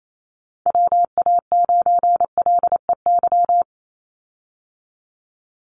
Morse code, heard as a keyed tone, about 28 wpm, 700 Hz. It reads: WA9LEY
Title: Call Signs Repeated 2x - 28wpm